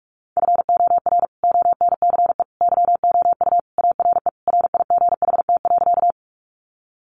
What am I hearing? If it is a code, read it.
Morse code, heard as a keyed tone, about 44 wpm, 700 Hz. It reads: FOR ONCE YOU ARE RIGHT.
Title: Sentences from Top 300 Words 45wpm